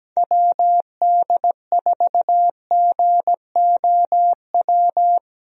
Morse code, heard as a keyed tone, about 17 wpm, 700 Hz. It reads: WD4GOW
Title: USA Call Signs - 6 Characters - 17wpm